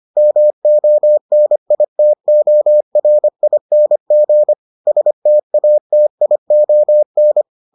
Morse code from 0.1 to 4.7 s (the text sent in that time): MONITORING